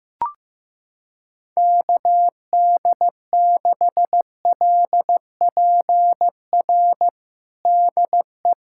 Morse code, heard as a keyed tone, about 15 wpm, 700 Hz. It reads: KD6LPR DE
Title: USA Callsign DE Callsign - 6 Characters - 15wpm